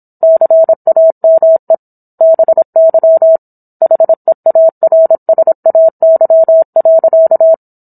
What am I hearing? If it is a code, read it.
Morse code, heard as a keyed tone, about 27 wpm, 650 Hz. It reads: CAME BY HEARSAY.